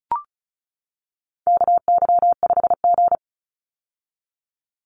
Morse code, heard as a keyed tone, about 34 wpm, 700 Hz. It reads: KY5G
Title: USA Call Signs 35wpm